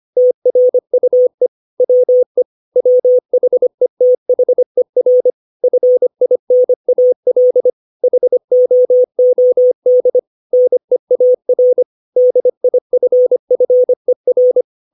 Morse code, heard as a keyed tone, about 25 wpm, 500 Hz. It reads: TRUE WE WHETHER FINAL HOOD NEAR DIFFER
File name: 632 words / 632 Words 25WPM